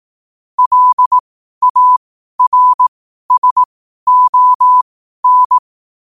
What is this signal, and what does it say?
Morse code, heard as a keyed tone, about 18 wpm, spaced as for 12 wpm, 1000 Hz. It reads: LARSON
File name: qotdfiles / 12wpm